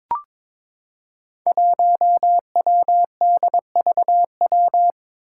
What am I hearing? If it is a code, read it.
Morse code, heard as a keyed tone, about 21 wpm, 700 Hz. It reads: 1WDVW
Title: Five Challenging Letter-Number Combinations 22wpm